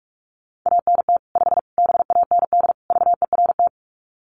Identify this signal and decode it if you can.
Morse code, heard as a keyed tone, about 44 wpm, 700 Hz. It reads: ANT 5 BAND VERT
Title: Common QSO Phrases 45wpm